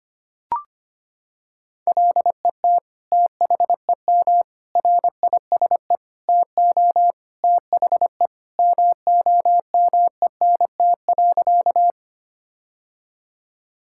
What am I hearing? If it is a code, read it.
Morse code, heard as a keyed tone, about 25 wpm, 700 Hz. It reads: LET THEM RISE TO THE MOMENT.
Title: Sentences from Top 1000 Words - Review Part 6 - 25wpm